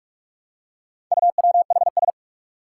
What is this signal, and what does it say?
Morse code, heard as a keyed tone, about 46 wpm, 700 Hz. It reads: UWHS
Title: Four Challenging Letter-Number Combinations 45wpm